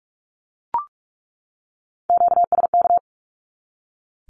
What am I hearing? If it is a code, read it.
Morse code, heard as a keyed tone, about 44 wpm, 700 Hz. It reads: QSK
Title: Q-Codes 45wpm